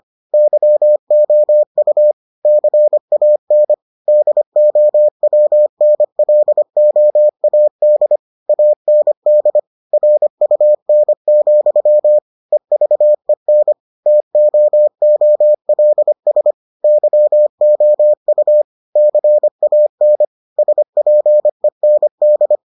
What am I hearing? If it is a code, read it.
Morse code, heard as a keyed tone, about 25 wpm, 600 Hz. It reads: YOU CAN DOWNLOAD AND RUN, EVEN TOOLS YOU CAN SPEND